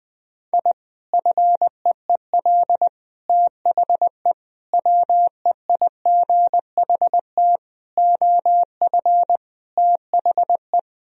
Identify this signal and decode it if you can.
Morse code, heard as a keyed tone, about 21 wpm, 700 Hz. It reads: I FEEL THE WEIGHT OF THE
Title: Sentences from Top 1000 Words - Review Part 3 - 20wpm